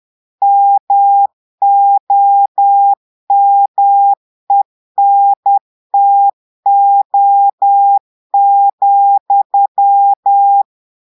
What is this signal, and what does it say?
Morse code, heard as a keyed tone, about 10 wpm, 800 Hz. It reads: MOMENTO,
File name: portugais texte en clair 2 vitesse 10 mots minutes